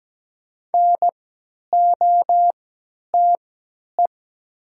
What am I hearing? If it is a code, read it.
Morse code, heard as a keyed tone, about 17 wpm, spaced as for 10 wpm, 700 Hz. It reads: NOTE
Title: 3x Character Spacing - Top 500 Words - 17wpm